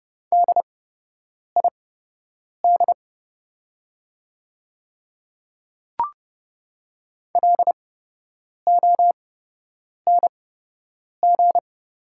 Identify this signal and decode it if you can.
Morse code, heard as a keyed tone, about 29 wpm, spaced as for 8 wpm, 700 Hz. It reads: DID LONG
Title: 8x Character Spacing - Top 100 Words - 30wpm